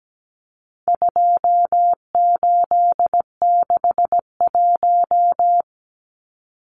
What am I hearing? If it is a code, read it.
Morse code, heard as a keyed tone, about 17 wpm, 700 Hz. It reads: 2861
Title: Numbers - 4-Digit 17wpm